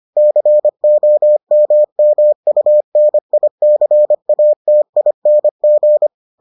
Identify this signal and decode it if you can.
Morse code, heard as a keyed tone, about 25 wpm, 600 Hz. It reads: COMMUNICATING